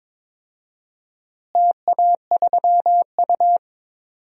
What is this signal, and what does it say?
Morse code, heard as a keyed tone, about 22 wpm, 700 Hz. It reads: TA3U
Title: DXCC-100 Call Signs - 4 Characters - Repeated 2x - 22wpm